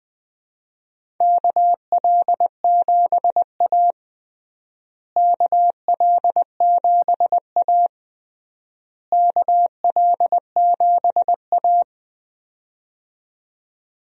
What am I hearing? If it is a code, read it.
Morse code, heard as a keyed tone, about 20 wpm, 700 Hz. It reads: KL7A KL7A KL7A
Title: DXCC-100 Call Signs - 4 Characters - Repeated 3x - 20wpm